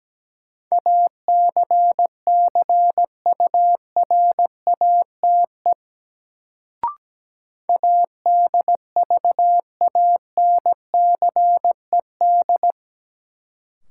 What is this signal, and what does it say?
Morse code, heard as a keyed tone, about 17 wpm, 700 Hz. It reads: ACCURATE ADVANCED
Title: Sentences from Top 1200 Words - New Word Drill Alt - 17wpm